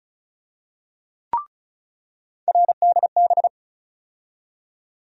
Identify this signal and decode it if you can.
Morse code, heard as a keyed tone, about 36 wpm, 700 Hz. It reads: RDB
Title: USA Call Sign Postfixes - 3 Letters 35wpm